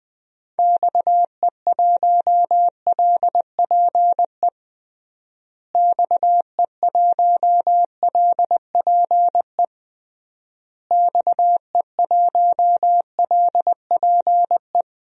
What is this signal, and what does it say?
Morse code, heard as a keyed tone, about 20 wpm, 700 Hz. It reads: XE1LPE XE1LPE XE1LPE
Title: DXCC-100 Call Signs - 6 Characters - Repeated 3x - 20wpm